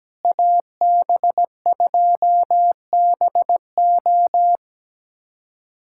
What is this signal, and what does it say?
Morse code, heard as a keyed tone, about 17 wpm, 700 Hz. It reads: AB2BO
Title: USA Callsign DE Callsign - 5 Characters - 17wpm